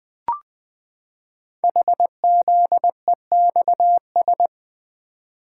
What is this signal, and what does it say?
Morse code, heard as a keyed tone, about 20 wpm, 700 Hz. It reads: HZEXS
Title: Code Groups - 5 Letters - 20wpm